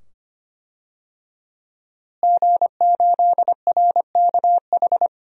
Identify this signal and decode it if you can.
Morse code, heard as a keyed tone, about 25 wpm, 700 Hz. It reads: G8RKH
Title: DXCC-100 Call Signs - 5 Characters - Repeated 2x - 25wpm